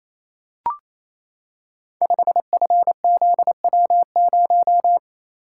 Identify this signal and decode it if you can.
Morse code, heard as a keyed tone, about 29 wpm, 700 Hz. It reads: 5FZW0
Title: Code Groups - 5 Letter:Numbers - 28wpm